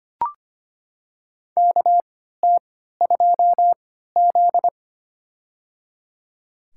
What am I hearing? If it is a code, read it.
Morse code, heard as a keyed tone, about 25 wpm, spaced as for 14 wpm, 700 Hz. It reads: KT2Z
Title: USA Call Signs - 3x Character Spacing - 25wpm